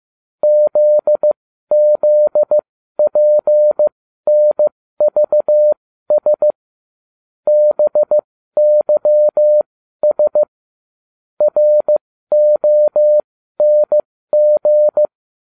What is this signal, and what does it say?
Morse code, heard as a keyed tone, about 15 wpm, spaced as for 12 wpm, 600 Hz. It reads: ZZPNVS BYS RONG